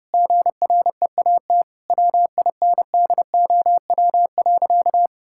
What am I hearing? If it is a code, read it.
Morse code, heard as a keyed tone, about 30 wpm, 700 Hz. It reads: GREAT WINDOW.
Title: Sentences from Top 1600 Words 30wpm